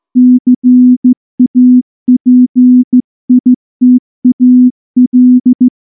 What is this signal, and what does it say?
Morse code, heard as a keyed tone, about 14 wpm, 250 Hz. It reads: CAPITAL
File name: morse-training-15WPM